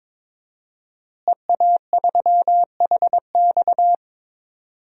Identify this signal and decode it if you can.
Morse code, heard as a keyed tone, about 22 wpm, 700 Hz. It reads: EA3HX
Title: DXCC-100 Call Signs - 5 Characters - Repeated 3x - 22wpm